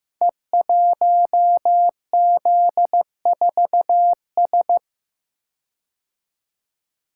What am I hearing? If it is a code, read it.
Morse code, heard as a keyed tone, about 15 wpm, 700 Hz. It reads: E1Z4S
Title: Code Groups - 5 Letter:Numbers - 15wpm